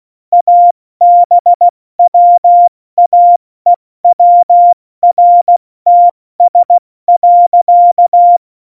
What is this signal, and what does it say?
Morse code, heard as a keyed tone, about 16 wpm, 700 Hz. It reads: ABWAEWRTS.
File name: Morsekurs / Witze / Witz 03.Juli 201, 70 F80